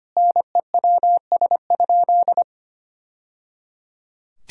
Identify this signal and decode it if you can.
Morse code, heard as a keyed tone, about 25 wpm, 690 Hz. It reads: NEWS?
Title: Sentences from Top 1000 Words - Review Part 4 - 25wpm